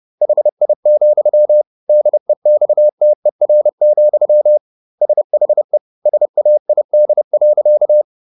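Morse code, heard as a keyed tone, about 30 wpm, 600 Hz. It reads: HI, DEXTER, SHE SAID.
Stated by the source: Morse code at 30 wpm, the novel Never by Ken Follett in English, from CW med Gnister.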